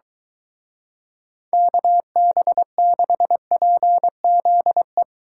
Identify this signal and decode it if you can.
Morse code, heard as a keyed tone, about 23 wpm, 700 Hz. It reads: KB6PZE